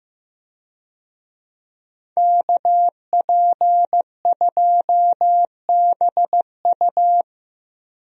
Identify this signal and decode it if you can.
Morse code, heard as a keyed tone, about 15 wpm, 700 Hz. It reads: KP2BU
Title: USA Call Signs - 5 Characters - Repeated 3x - 15wpm